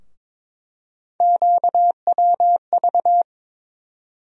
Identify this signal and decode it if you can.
Morse code, heard as a keyed tone, about 21 wpm, 700 Hz. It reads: QWV
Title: USA Call Sign Postfixes - 3 Letters 22wpm